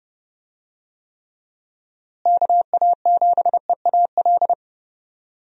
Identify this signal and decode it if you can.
Morse code, heard as a keyed tone, about 30 wpm, 700 Hz. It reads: KA7EAL
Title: USA Call Signs - 6 Characters - Repeated 3x - 30wpm